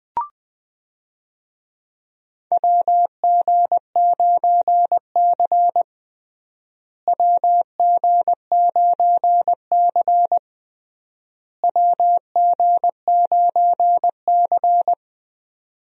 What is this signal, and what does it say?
Morse code, heard as a keyed tone, about 21 wpm, 700 Hz. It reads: WG9C WG9C WG9C
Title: USA Call Signs - 4 Characters - Repeated 3x - 20wpm